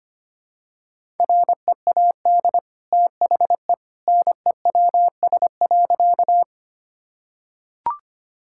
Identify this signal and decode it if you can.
Morse code, heard as a keyed tone, about 25 wpm, 700 Hz. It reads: READ THE NEWS.
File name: Easy Sentences with Hints - 25wpm